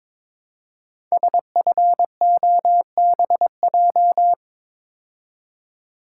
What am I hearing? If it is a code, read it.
Morse code, heard as a keyed tone, about 22 wpm, 700 Hz. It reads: SFOBJ
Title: Code Groups - 5 Letters - 22wpm